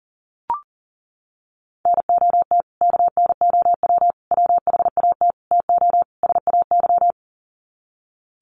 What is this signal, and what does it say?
Morse code, heard as a keyed tone, about 40 wpm, 700 Hz. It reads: NOT KNOW WHAT TO SAY
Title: Sets of 5 Words 40wpm